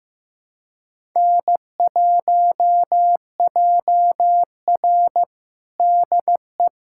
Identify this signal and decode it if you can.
Morse code, heard as a keyed tone, about 16 wpm, 700 Hz. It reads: N1JR DE
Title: USA Callsign DE Callsign - 4 Characters - 15wpm